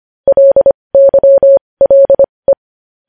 Morse code, heard as a keyed tone, about 25 wpm, spaced as for 20 wpm, 550 Hz. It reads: LYLE
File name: podcast / News headlines-1x-25x20